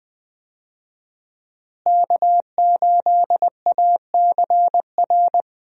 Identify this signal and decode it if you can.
Morse code, heard as a keyed tone, about 20 wpm, 700 Hz. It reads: K8ACR
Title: USA Call Signs - 5 Characters - Repeated 2x - 20wpm